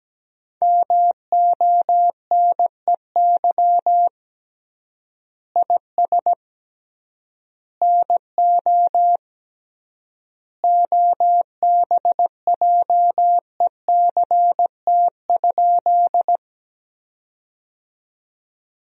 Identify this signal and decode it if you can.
Morse code, heard as a keyed tone, about 17 wpm, 700 Hz. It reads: MONEY IS NO OBJECT?
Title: 3x Word-Spacing - Sentences from Top 500 Words - 17wpm